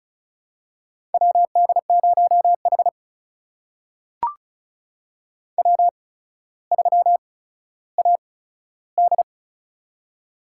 Morse code, heard as a keyed tone, about 36 wpm, spaced as for 15 wpm, 700 Hz. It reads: WD0H W3AD
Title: USA Call Signs - 4 Characters - 8x Character Spacing - 35wpm